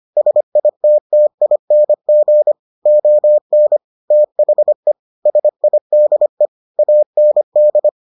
Morse code, heard as a keyed tone, about 25 wpm, 600 Hz. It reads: SITTING ON THE SIDE AND